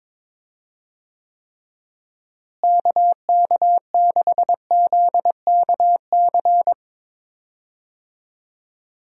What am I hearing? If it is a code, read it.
Morse code, heard as a keyed tone, about 22 wpm, 700 Hz. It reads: KK6ZKC
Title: USA Call Signs - 6 Characters - Repeated 3x - 22wpm